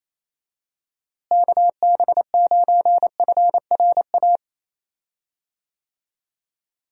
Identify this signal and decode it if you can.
Morse code, heard as a keyed tone, about 28 wpm, 700 Hz. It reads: KB9FRA
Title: USA Call Signs - 6 Characters - Repeated 3x - 28wpm